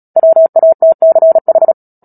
Morse code, heard as a keyed tone, about 37 wpm, 650 Hz. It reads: WATCH
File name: Book Two / 10-Chapter 10 - Epilogue - 36 WPM